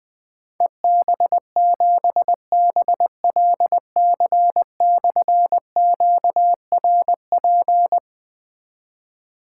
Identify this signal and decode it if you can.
Morse code, heard as a keyed tone, about 20 wpm, 700 Hz. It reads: EB7BLC/QRP
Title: Difficult Call Signs 20wpm